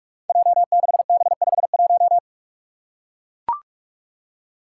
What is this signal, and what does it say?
Morse code, heard as a keyed tone, about 41 wpm, 700 Hz. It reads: J6B51
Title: Five Challenging Letter-Number Combinations 45wpm